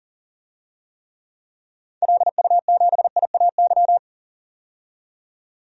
Morse code, heard as a keyed tone, about 40 wpm, 700 Hz. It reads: LU7IAY
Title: DXCC-100 Call Signs - 6 Characters - 40wpm